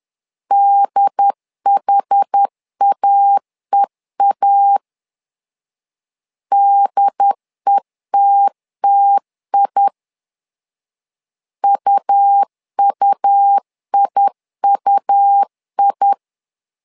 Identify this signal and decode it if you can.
Morse code, heard as a keyed tone, about 10 wpm, 800 Hz. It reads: DHAEA DETTI UUIUI
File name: LEZIONI / 2 UHDN / 2 UHD 10wpm 1826